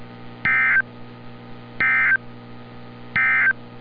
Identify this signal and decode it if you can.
Emergency Alert System (SAME) tones, heard as harsh buzzing bursts. It is heard from 0.3 to 3.6 s.
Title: NOAA Weather Radio - Altoona, PA